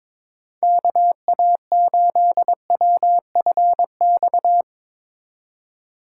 Morse code, heard as a keyed tone, about 22 wpm, 700 Hz. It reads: KA8WFX
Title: Speed Racing - USA 2x3 Call Signs 22wpm